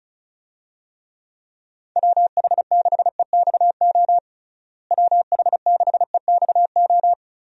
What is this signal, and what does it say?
Morse code, heard as a keyed tone, about 34 wpm, 700 Hz. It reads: WH6EXO WH6EXO
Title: USA Call Signs - 6 Characters - Repeated 3x - 35wpm